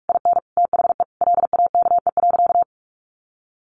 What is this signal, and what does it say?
Morse code, heard as a keyed tone, about 46 wpm, 700 Hz. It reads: IN THE LAKE.
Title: Sentences from Top 1100 Words 45wpm